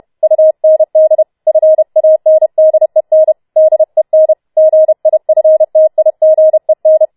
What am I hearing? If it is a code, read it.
Morse code, heard as a keyed tone, about 31 wpm, 630 Hz. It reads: UND FANDEN DEN GIFTIGEN